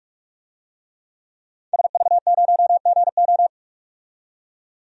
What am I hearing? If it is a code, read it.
Morse code, heard as a keyed tone, about 46 wpm, 700 Hz. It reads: SV0GO
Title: DXCC-100 Call Signs - 5 Characters - 45wpm